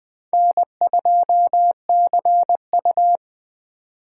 Morse code, heard as a keyed tone, about 24 wpm, 700 Hz. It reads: N2CU
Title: Speed Racing - USA 1x2 Call Signs 20wpm